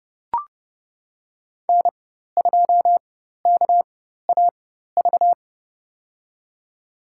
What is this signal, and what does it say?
Morse code, heard as a keyed tone, about 32 wpm, spaced as for 14 wpm, 700 Hz. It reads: N2KAV
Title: USA Call Signs - 5 Characters - 4x Character Spacing - 30wpm